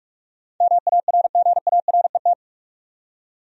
Morse code, heard as a keyed tone, about 46 wpm, 700 Hz. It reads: MARGARET
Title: USA Common Names - 45wpm